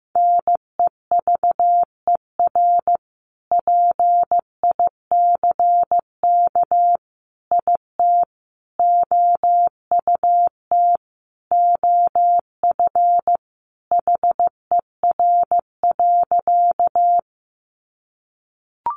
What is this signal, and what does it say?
Morse code, heard as a keyed tone, about 15 wpm, 700 Hz. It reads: NEVER PICK IT OUT OF HER.
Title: Sentences from Top 1000 Words 15wpm